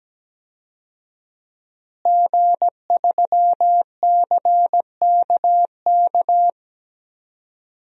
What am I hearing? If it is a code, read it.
Morse code, heard as a keyed tone, about 17 wpm, 700 Hz. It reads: G3CKK
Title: DXCC-100 Call Signs - 5 Characters - Repeated 2x - 17wpm